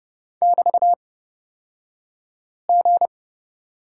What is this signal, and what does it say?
Morse code, heard as a keyed tone, about 30 wpm, spaced as for 5 wpm, 700 Hz. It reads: <BT>G
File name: Lesson 37d - Letter x - Two Characters 30wpm